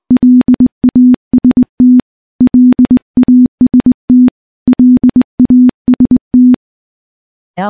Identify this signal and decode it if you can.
Morse code, heard as a keyed tone, about 20 wpm, 250 Hz. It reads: LAST LAST LAST
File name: morse-training-20WPM